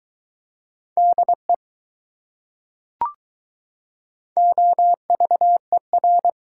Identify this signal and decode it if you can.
Morse code, heard as a keyed tone, about 20 wpm, 700 Hz. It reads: DE OVER